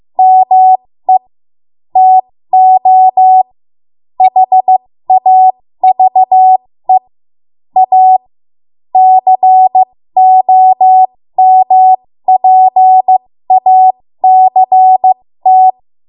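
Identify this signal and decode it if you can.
Morse code, heard as a keyed tone, about 15 wpm, spaced as for 13 wpm, 750 Hz. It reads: ME TO HAVE A COMPACT
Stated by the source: 13 wpm Morse code practice archive transmission.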